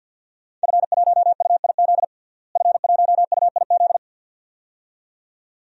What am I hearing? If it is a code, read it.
Morse code, heard as a keyed tone, about 52 wpm, 700 Hz. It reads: F1UIZ F1UIZ
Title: DXCC-100 Call Signs - 5 Characters - Repeated 2x - 50wpm